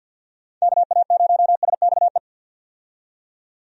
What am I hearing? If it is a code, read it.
Morse code, heard as a keyed tone, about 50 wpm, 700 Hz. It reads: KA0SXE
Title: USA Call Signs - 6 Characters - 50wpm